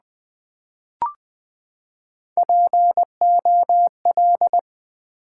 Morse code, heard as a keyed tone, about 20 wpm, 690 Hz. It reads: POL
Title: Build A Word - 3 Letters At A Time - 9-Letter Words - 20wpm